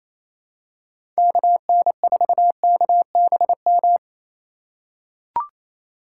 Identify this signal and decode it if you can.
Morse code, heard as a keyed tone, about 27 wpm, 700 Hz. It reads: KN4KBM
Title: USA Call Signs - 6x Character Spacing - 28wpm